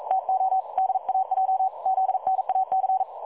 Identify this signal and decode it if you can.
Morse code, heard as a keyed tone, about 42 wpm, 760 Hz. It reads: TO DAJ ZNAC
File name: dlazaawansowanych / nagraniazpasma / websdr recording 3565.0kHz